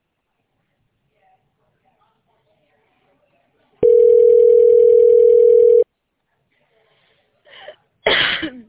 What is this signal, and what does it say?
Ringback tone (North America), 440 + 480 Hz.